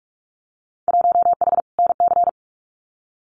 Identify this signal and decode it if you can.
Morse code, heard as a keyed tone, about 44 wpm, 700 Hz. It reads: 1H NC
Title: Winter Field Day Exchanges 45wpm